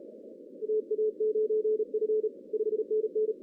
Morse code, heard as a keyed tone, about 33 wpm, 430 Hz. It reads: TA9F 5NN
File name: 2015 cq ww cw / 2015 cqwww k5tr - 0-18